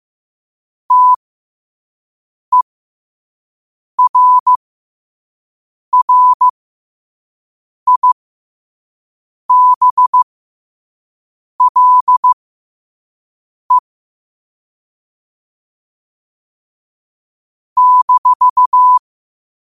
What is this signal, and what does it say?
Morse code, heard as a keyed tone, about 15 wpm, spaced as for 5 wpm, 1000 Hz. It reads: TERRIBLE -
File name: qotdfiles / 5wpm